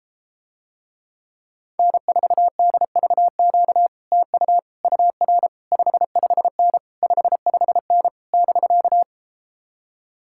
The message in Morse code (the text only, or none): N4DVQ TU UR 55N 55N <BK>